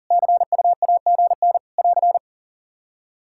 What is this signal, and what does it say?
Morse code, heard as a keyed tone, about 39 wpm, 700 Hz. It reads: CUAGN <AR>
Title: Practice QSO - Exchange 4 - Ending - 40wpm